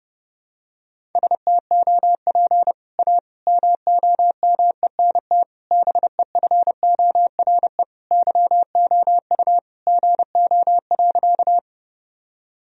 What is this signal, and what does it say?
Morse code, heard as a keyed tone, about 32 wpm, 700 Hz. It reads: STOP A MOMENT BEFORE YOU GO.
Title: Sentences from Top 1000 Words - Review Part 1 - 30wpm